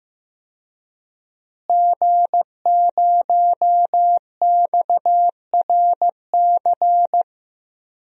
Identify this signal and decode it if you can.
Morse code, heard as a keyed tone, about 15 wpm, 700 Hz. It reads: G0XRC
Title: DXCC-100 Call Signs - 5 Characters - Repeated 2x - 15wpm